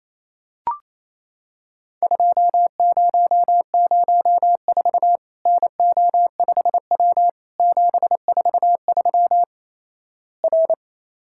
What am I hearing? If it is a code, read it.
Morse code, heard as a keyed tone, about 28 wpm, 700 Hz. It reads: 2004 NO5W 743 R
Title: Worked All Europe DX Contest - QTC Exchange - 28wpm